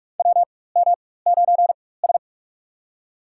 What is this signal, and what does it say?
Morse code, heard as a keyed tone, about 43 wpm, spaced as for 21 wpm, 700 Hz. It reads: WM9S
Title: USA Call Signs - 4x Character Spacing - 45wpm